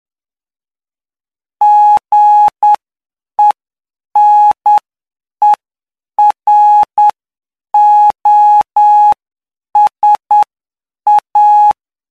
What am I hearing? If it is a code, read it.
Morse code, heard as a keyed tone, about 9 wpm, spaced as for 7 wpm, 840 Hz. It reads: GENEROSA
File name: cw / py2lcd / 08-PY2LCD-Li��o 17